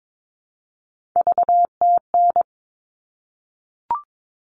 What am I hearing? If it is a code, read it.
Morse code, heard as a keyed tone, about 22 wpm, 700 Hz. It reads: VTN
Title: All Cut Numbers - 3-Digit 22wpm